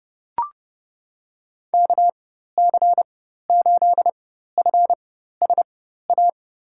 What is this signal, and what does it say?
Morse code, heard as a keyed tone, about 32 wpm, spaced as for 14 wpm, 700 Hz. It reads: KC8FSA
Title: USA Call Signs - 4x Character Spacing - 30wpm